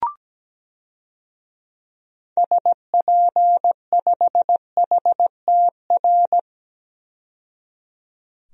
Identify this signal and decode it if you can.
Morse code, heard as a keyed tone, about 17 wpm, 700 Hz. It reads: SP5HTR
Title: DXCC-100 Call Signs - 6 Characters - 17wpm